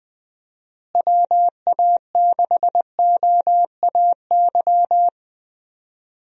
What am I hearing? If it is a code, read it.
Morse code, heard as a keyed tone, about 20 wpm, 700 Hz. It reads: WA6OAY
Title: USA Call Signs - 6 Characters - Repeated 2x - 20wpm